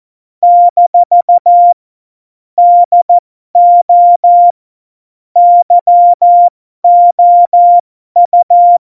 Morse code, heard as a keyed tone, about 14 wpm, spaced as for 12 wpm, 700 Hz. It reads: - DO YOU
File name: Morsekurs / Witze